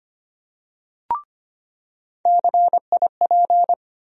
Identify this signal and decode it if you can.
Morse code, heard as a keyed tone, about 25 wpm, 700 Hz. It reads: CIP
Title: USA Call Sign Postfixes - 3 Letters 25wpm